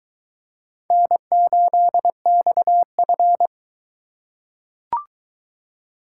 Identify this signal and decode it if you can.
Morse code, heard as a keyed tone, about 21 wpm, 700 Hz. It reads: N8XF